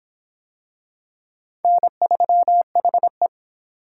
Morse code, heard as a keyed tone, about 23 wpm, 700 Hz. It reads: N3HE